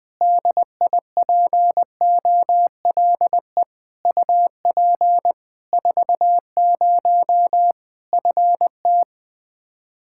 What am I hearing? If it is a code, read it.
Morse code, heard as a keyed tone, about 20 wpm, 700 Hz. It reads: DIPOLE UP 40 FT
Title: K1USN SST QSO Practice 20wpm